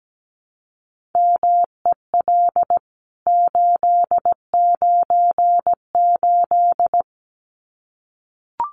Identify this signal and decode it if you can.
Morse code, heard as a keyed tone, about 17 wpm, 700 Hz. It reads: MEL 898
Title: CWT Exchange 17wpm